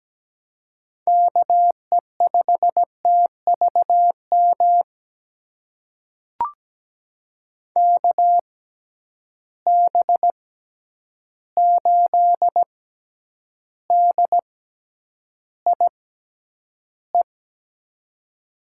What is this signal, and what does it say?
Morse code, heard as a keyed tone, about 17 wpm, spaced as for 9 wpm, 700 Hz. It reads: KE5TVM KB8DIE
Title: USA Call Signs - 6 Characters - 6x Character Spacing - 17wpm